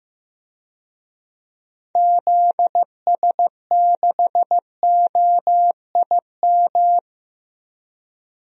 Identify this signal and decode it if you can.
Morse code, heard as a keyed tone, about 15 wpm, 700 Hz. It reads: ZS6OIM
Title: DXCC-100 Call Signs - 6 Characters - 15wpm